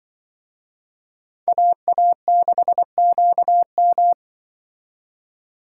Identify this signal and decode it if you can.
Morse code, heard as a keyed tone, about 24 wpm, 700 Hz. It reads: AA6QM
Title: Speed Racing - USA 2x2 Call Signs 20wpm